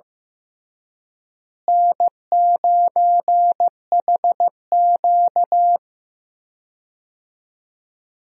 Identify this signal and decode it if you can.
Morse code, heard as a keyed tone, about 15 wpm, 700 Hz. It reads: N9HQ
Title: USA Call Signs - 4 Characters - Repeated 2x - 15wpm